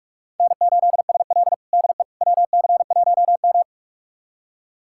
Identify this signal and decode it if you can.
Morse code, heard as a keyed tone, about 45 wpm, 700 Hz. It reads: N8SL DE WC1M
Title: USA Callsign DE Callsign - 4 Characters - 45wpm